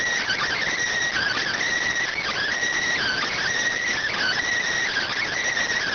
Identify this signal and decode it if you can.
Emergency Alert System (SAME) tones (header bursts), heard as harsh buzzing bursts. It is heard from 1.1 to 5.5 s.